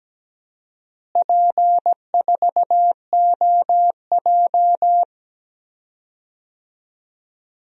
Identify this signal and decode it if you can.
Morse code, heard as a keyed tone, about 17 wpm, 700 Hz. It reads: P4OJ
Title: DXCC-100 Call Signs - 4 Characters - Repeated 2x - 17wpm